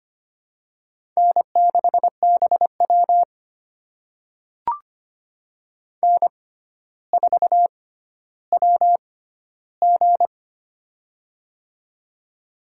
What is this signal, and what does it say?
Morse code, heard as a keyed tone, about 25 wpm, spaced as for 13 wpm, 700 Hz. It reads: N6BW N4WG
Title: USA Call Signs - 4 Characters - 6x Character Spacing - 25wpm